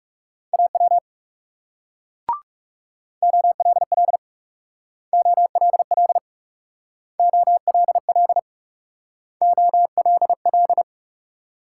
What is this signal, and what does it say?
Morse code, heard as a keyed tone, about 36 wpm, 700 Hz. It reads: AW OLL OLL OLL OLL
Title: Speed-Racing Multi-letter Phonemes - 30wpm